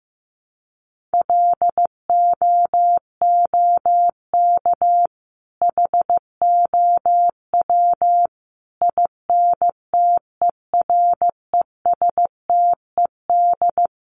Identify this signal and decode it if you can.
Morse code, heard as a keyed tone, about 15 wpm, 700 Hz. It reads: LOOK HOW INTERESTED
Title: Sentences from Top 1400 Words 15wpm